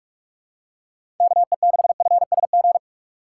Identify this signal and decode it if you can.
Morse code, heard as a keyed tone, about 46 wpm, 700 Hz. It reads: KE6FSG
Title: USA Call Signs - 3x Character Spacing - 45wpm